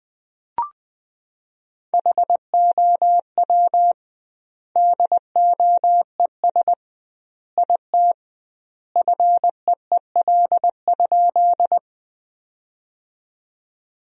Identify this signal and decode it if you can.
Morse code, heard as a keyed tone, about 20 wpm, 700 Hz. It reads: HOW DOES IT FEEL?
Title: 2x Word-Spacing - Sentences from Top 400 Words - 20wpm